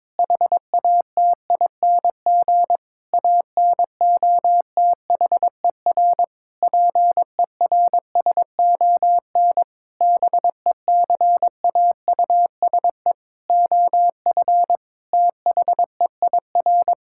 Morse code, heard as a keyed tone, about 22 wpm, 700 Hz. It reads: HATING ANOTHER PERSON BECAUSE OF THEIR
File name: Sentences with 9-Word Phrases - 22wpm